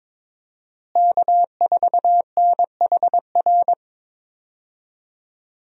Morse code, heard as a keyed tone, about 22 wpm, 700 Hz. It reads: K4NHR
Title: USA Call Signs - 5 Characters - Repeated 3x - 22wpm